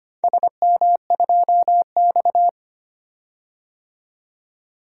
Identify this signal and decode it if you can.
Morse code, heard as a keyed tone, about 25 wpm, 700 Hz. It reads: SM2X
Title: DXCC-100 Call Signs - 4 Characters - 25wpm